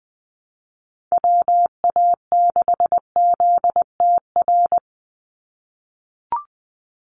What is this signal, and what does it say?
Morse code, heard as a keyed tone, about 20 wpm, 700 Hz. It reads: WA6ZTR
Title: USA Call Signs 20wpm